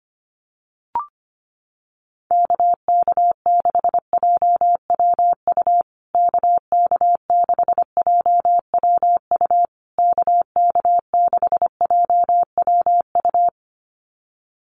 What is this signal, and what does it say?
Morse code, heard as a keyed tone, about 25 wpm, 700 Hz. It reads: KK6JWU KK6JWU KK6JWU
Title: USA Call Signs Repeated 3x 25wpm